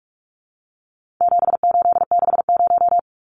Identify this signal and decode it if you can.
Morse code, heard as a keyed tone, about 44 wpm, 700 Hz. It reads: 7860
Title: Numbers - 4-Digit 45wpm